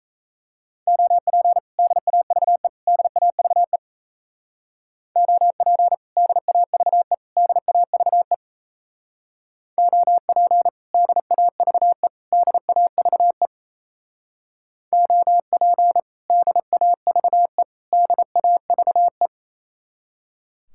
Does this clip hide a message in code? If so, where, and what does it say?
Yes: Morse code from 0.9 to 19.3 s, heard as a keyed tone, about 35 wpm, 700 Hz. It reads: OP DAVE DAVE OP DAVE DAVE OP DAVE DAVE OP DAVE DAVE